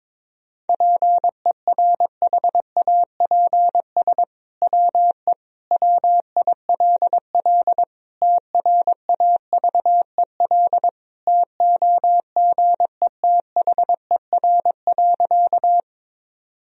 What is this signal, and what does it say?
Morse code, heard as a keyed tone, about 22 wpm, 700 Hz. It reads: PERHAPS WE WILL TRAVEL TOGETHER.
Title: Sentences from Top 1000 Words - Review Part 2 - 22wpm